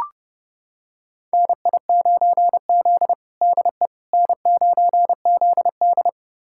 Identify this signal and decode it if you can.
Morse code, heard as a keyed tone, about 32 wpm, 700 Hz. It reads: NI9Z DE N9ZD
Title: USA Callsign DE Callsign - 4 Characters - 30wpm